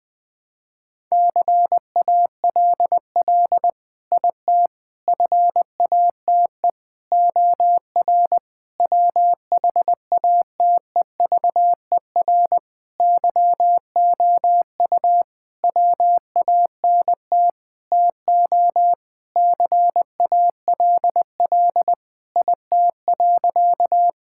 Morse code, heard as a keyed tone, about 20 wpm, 700 Hz. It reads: CALL IT FATE OR WHATEVER YOU WANT TO CALL IT.
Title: Sentences with 7-Word Phrases - Spoken Beginning - 20wpm